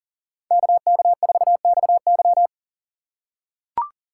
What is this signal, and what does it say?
Morse code, heard as a keyed tone, about 42 wpm, 690 Hz. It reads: KK4XY
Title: USA Call Signs - 5 Characters - 6x Character Spacing - 40wpm